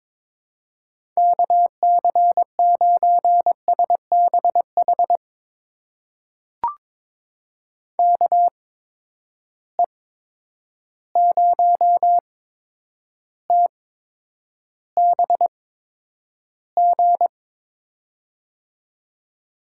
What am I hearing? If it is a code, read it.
Morse code, heard as a keyed tone, about 22 wpm, spaced as for 9 wpm, 700 Hz. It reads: KC9SBH KE0TBG
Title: USA Call Signs - 6 Characters - 8x Character Spacing - 22wpm